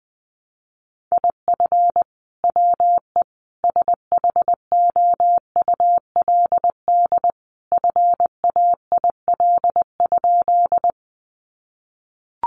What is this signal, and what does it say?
Morse code, heard as a keyed tone, about 20 wpm, 700 Hz. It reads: IF WE SHOULD FAIL?
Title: Sentences from Top 1100 Words 20wpm